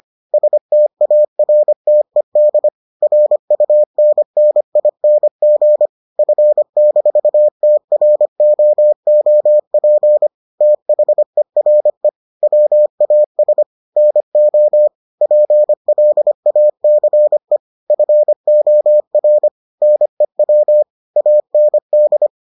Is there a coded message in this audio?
Yes: Morse code from 0.3 to 22.3 s, heard as a keyed tone, about 25 wpm, 600 Hz. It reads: STARTED RUNNING F-TROOP THERE WAS NO PLACE FOR NEW AND